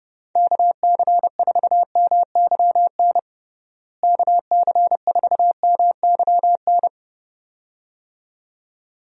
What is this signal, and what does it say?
Morse code, heard as a keyed tone, about 30 wpm, 700 Hz. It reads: KC4MYN KC4MYN
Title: USA Call Signs - 6 Characters - Repeated 2x - 30wpm